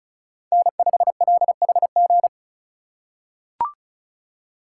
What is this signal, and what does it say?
Morse code, heard as a keyed tone, about 36 wpm, 700 Hz. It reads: N5LHG
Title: USA Call Signs - 5 Characters - 3x Character Spacing - 35wpm